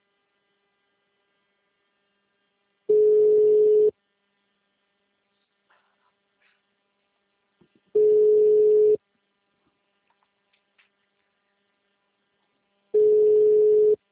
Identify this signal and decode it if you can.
Ringback tone (Europe, 425 Hz).
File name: out- -2025- - - .7565